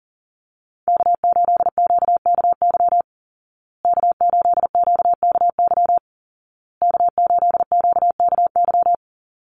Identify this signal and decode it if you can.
Morse code, heard as a keyed tone, about 38 wpm, 700 Hz. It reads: K8QKY K8QKY K8QKY
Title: Call Signs Repeated 2x - 40wpm